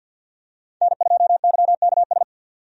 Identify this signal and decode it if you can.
Morse code, heard as a keyed tone, about 53 wpm, 700 Hz. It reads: N2YKS
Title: USA Call Signs - 5 Characters - 50wpm